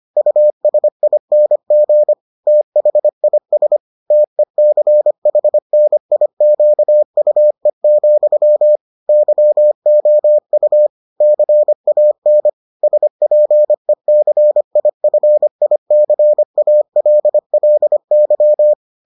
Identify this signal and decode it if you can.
Morse code, heard as a keyed tone, about 25 wpm, 600 Hz. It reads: USING THIS TECHNIQUE, YOU CAN SPECIFICALLY